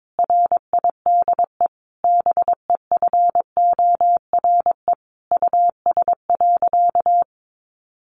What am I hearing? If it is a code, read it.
Morse code, heard as a keyed tone, about 22 wpm, 700 Hz. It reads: RIDE BEFORE US.
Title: Sentences from Top 1200 Words 22wpm